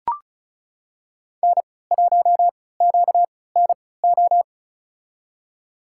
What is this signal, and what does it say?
Morse code, heard as a keyed tone, about 34 wpm, spaced as for 20 wpm, 700 Hz. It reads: N1QNO
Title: USA Call Signs - 5 Characters - 3x Character Spacing - 35wpm